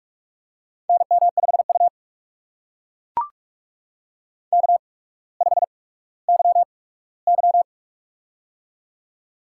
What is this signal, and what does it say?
Morse code, heard as a keyed tone, about 44 wpm, spaced as for 19 wpm, 700 Hz. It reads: NM5U K5YY
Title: USA Call Signs - 4 Characters - 8x Character Spacing - 45wpm